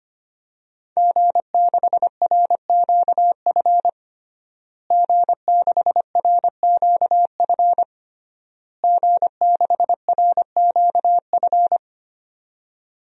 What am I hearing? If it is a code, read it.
Morse code, heard as a keyed tone, about 25 wpm, 700 Hz. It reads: G6RQF G6RQF G6RQF
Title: DXCC-100 Call Signs - 5 Characters - Repeated 3x - 25wpm